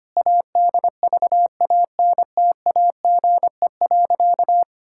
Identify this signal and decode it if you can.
Morse code, heard as a keyed tone, about 25 wpm, 700 Hz. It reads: ADVANTAGE.
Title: Sentences from Top 1000 Words - Review Part 6 - 25wpm